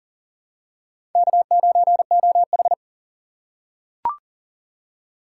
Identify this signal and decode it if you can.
Morse code, heard as a keyed tone, about 40 wpm, 700 Hz. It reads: K9OH
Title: USA Call Signs - 2x Character Spacing - 40wpm